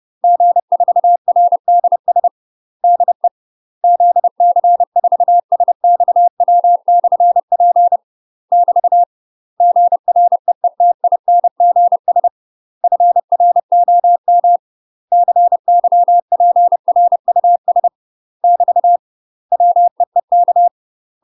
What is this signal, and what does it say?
Morse code, heard as a keyed tone, about 30 wpm, 700 Hz. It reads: G4RDS DE ZC4SXW/P <BT> GREETINGS FROM CYPRUS <BT> WEEK
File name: UK MASTER QSOs TRAINING / 30WPM / UK MASTER QSOs 0023